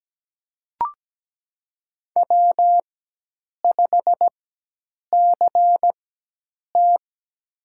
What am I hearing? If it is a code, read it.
Morse code, heard as a keyed tone, about 17 wpm, spaced as for 8 wpm, 700 Hz. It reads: W5CT
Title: USA Call Signs - 4 Characters - 4x Character Spacing - 17wpm